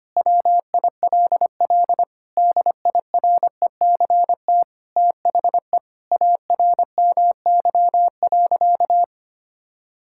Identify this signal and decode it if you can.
Morse code, heard as a keyed tone, about 25 wpm, 700 Hz. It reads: WILL DIRECT THE ARMY.
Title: Sentences from Top 1000 Words - Review Part 5 - 25wpm